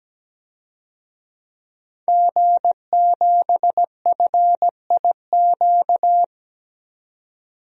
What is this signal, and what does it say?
Morse code, heard as a keyed tone, about 17 wpm, 700 Hz. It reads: G7FIQ
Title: DXCC-100 Call Signs - 5 Characters - 17wpm